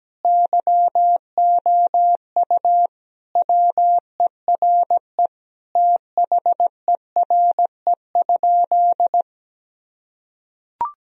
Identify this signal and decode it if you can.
Morse code, heard as a keyed tone, about 17 wpm, 700 Hz. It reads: YOU WERE THERE?
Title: Sentences from Top 1000 Words - Review Part 1 - 17wpm